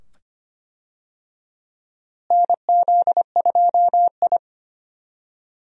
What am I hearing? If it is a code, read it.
Morse code, heard as a keyed tone, about 25 wpm, 700 Hz. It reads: NZ2I
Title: USA Call Signs - 4 Characters - Repeated 2x - 25wpm